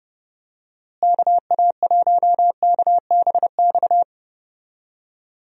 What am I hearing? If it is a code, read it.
Morse code, heard as a keyed tone, about 30 wpm, 700 Hz. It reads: KA1KBX
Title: USA Call Signs - 6 Characters - Repeated 2x - 30wpm